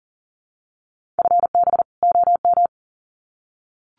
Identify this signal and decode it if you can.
Morse code, heard as a keyed tone, about 40 wpm, 700 Hz. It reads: FB OM
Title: Common QSO Phrases 40wpm